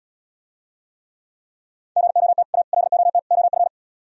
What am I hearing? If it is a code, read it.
Morse code, heard as a keyed tone, about 77 wpm, 700 Hz. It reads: GOT A LOT OF